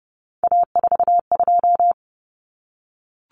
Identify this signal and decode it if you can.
Morse code, heard as a keyed tone, about 28 wpm, 700 Hz. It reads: A42
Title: Common Cut Numbers - 3-Digit 30wpm